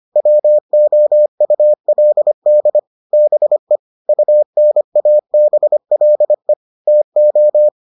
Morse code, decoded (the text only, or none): WOULD BE UNABLE TO